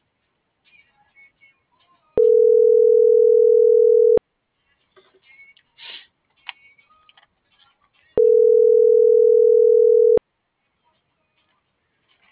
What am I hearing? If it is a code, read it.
Ringback tone (North America), 440 + 480 Hz.